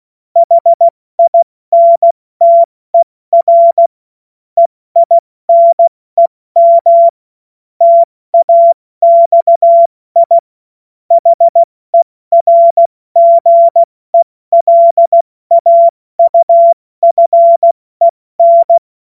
Morse code, heard as a keyed tone, about 16 wpm, spaced as for 14 wpm, 680 Hz. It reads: HINTER EINEM TAXI HERGELAUFEN